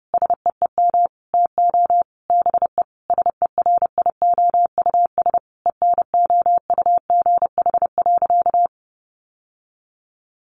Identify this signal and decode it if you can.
Morse code, heard as a keyed tone, about 30 wpm, 700 Hz. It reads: SEEM TO BE SERIOUS ENOUGH.
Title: Sentences from Top 1100 Words 30wpm